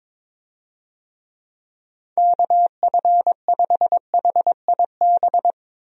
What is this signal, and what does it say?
Morse code, heard as a keyed tone, about 22 wpm, 700 Hz. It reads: KF5HIB
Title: USA Call Signs - 6 Characters - 22wpm